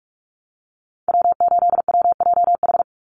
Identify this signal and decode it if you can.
Morse code, heard as a keyed tone, about 44 wpm, 700 Hz. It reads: W8WJH
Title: USA Call Signs 45wpm